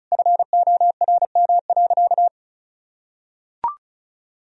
Morse code, heard as a keyed tone, about 34 wpm, 700 Hz. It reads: FORM.
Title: Sentences from Top 1000 Words - Review Part 1 - 35wpm